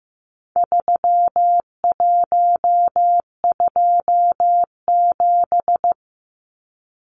Morse code, heard as a keyed tone, about 15 wpm, 700 Hz. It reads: 3127
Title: Numbers - 4-Digit 15wpm